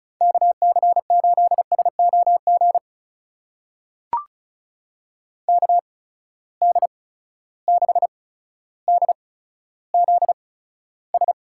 Morse code, heard as a keyed tone, about 36 wpm, spaced as for 15 wpm, 700 Hz. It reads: KC8SOG KD6DZS
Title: USA Call Signs - 6 Characters - 8x Character Spacing - 35wpm